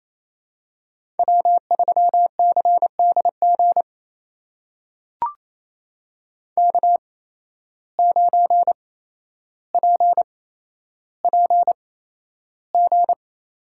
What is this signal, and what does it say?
Morse code, heard as a keyed tone, about 28 wpm, spaced as for 12 wpm, 700 Hz. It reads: W3CDG K9PPG
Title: USA Call Signs - 5 Characters - 8x Character Spacing - 28wpm